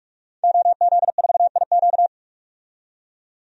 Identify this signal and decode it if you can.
Morse code, heard as a keyed tone, about 47 wpm, 700 Hz. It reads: OZ4IQ
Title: DXCC-100 Call Signs - 5 Characters - 45wpm